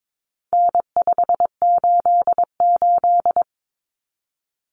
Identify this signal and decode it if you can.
Morse code, heard as a keyed tone, about 22 wpm, 700 Hz. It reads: N588
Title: Common Cut Numbers - 4-Digit 22wpm